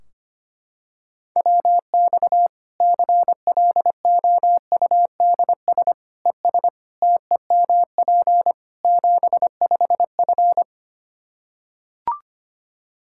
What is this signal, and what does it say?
Morse code, heard as a keyed tone, about 25 wpm, 700 Hz. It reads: WX CLOUDS ES TEMP 75F
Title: K1USN SST QSO Practice 25wpm